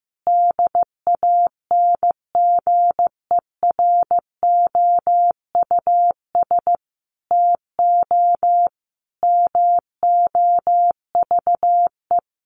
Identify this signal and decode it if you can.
Morse code, heard as a keyed tone, about 15 wpm, 700 Hz. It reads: DANGEROUS TO MOVE
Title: Sentences from Top 1200 Words 15wpm